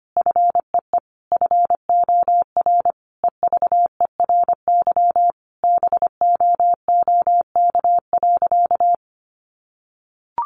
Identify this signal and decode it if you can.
Morse code, heard as a keyed tone, about 25 wpm, 700 Hz. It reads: FEE FOR EVERY BOOK.
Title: Sentences from Top 1500 Words 25wpm